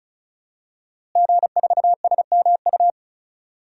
Morse code, heard as a keyed tone, about 34 wpm, 690 Hz. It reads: G4SMU
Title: DXCC-100 Call Signs - 5 Characters - Repeated 3x - 35wpm